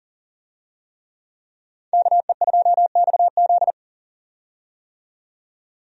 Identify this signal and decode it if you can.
Morse code, heard as a keyed tone, about 37 wpm, 700 Hz. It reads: KE2XZ
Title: USA Call Signs - 5 Characters - 40wpm